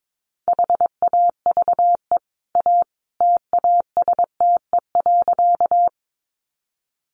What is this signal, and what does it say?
Morse code, heard as a keyed tone, about 22 wpm, 700 Hz. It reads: HAVE A TASTE.
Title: Sentences from Top 1100 Words 22wpm